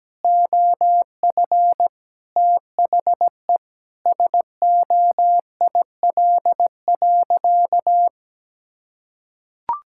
Morse code, heard as a keyed tone, about 17 wpm, 700 Hz. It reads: OF THE SOIL.
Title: Sentences from Top 1000 Words - Review Part 5 - 17wpm